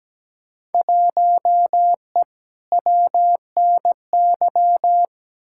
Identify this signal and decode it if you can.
Morse code, heard as a keyed tone, about 17 wpm, 700 Hz. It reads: 1E WNY
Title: Field Day Exchanges - 17wpm